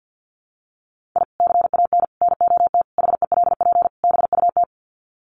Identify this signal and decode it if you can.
Morse code, heard as a keyed tone, about 50 wpm, 700 Hz. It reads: I CAN NOT HELP BUT
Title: Sets of 5 Words Encore2 50wpm